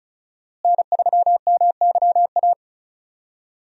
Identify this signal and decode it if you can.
Morse code, heard as a keyed tone, about 34 wpm, 700 Hz. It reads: N3MYA
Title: USA Call Signs - 5 Characters - Repeated 2x - 35wpm